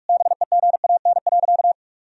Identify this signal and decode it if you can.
Morse code, heard as a keyed tone, about 41 wpm, 700 Hz. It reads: BEGAN.
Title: Sentences from Top 1000 Words - Review Part 1 - 45wpm